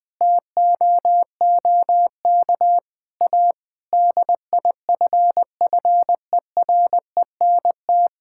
Morse code, heard as a keyed tone, about 21 wpm, 700 Hz. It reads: TOOK A DIFFERENT
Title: Sentences from Top 1600 Words 20wpm